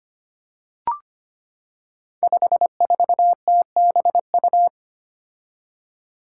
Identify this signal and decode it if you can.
Morse code, heard as a keyed tone, about 25 wpm, 700 Hz. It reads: 54TBU
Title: Five Challenging Letter-Number Combinations 25wpm